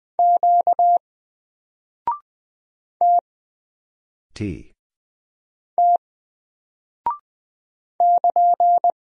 Morse code, heard as a keyed tone, about 20 wpm, 700 Hz. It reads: Q T T <KN>
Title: Single Character (Letter:Number:Punctuation:Prosign) 20wpm